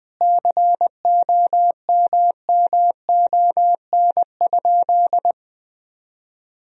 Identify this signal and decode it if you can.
Morse code, heard as a keyed tone, about 20 wpm, 700 Hz. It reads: COMMON?
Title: Sentences from Top 1000 Words - Review Part 3 - 20wpm